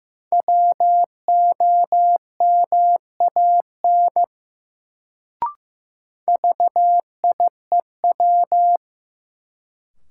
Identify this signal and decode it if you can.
Morse code, heard as a keyed tone, about 15 wpm, 700 Hz. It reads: WOMAN VIEW
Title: Sets of 5 Words - New Word Drill - 15wpm